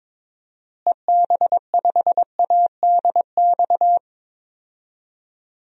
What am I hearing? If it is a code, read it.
Morse code, heard as a keyed tone, about 22 wpm, 700 Hz. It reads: EB5ADX
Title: DXCC-100 Call Signs - 6 Characters - Repeated 2x - 22wpm